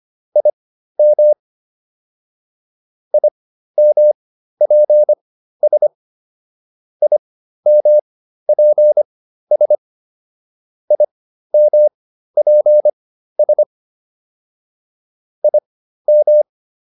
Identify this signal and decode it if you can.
Morse code, heard as a keyed tone, about 25 wpm, spaced as for 13 wpm, 600 Hz. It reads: IM IMPS IMPS IMPS IM